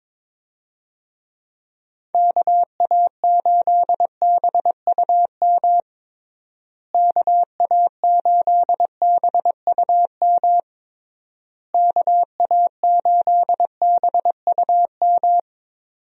Morse code, heard as a keyed tone, about 22 wpm, 700 Hz. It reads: KA8BUM KA8BUM KA8BUM
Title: USA Call Signs - 6 Characters - Repeated 3x - 22wpm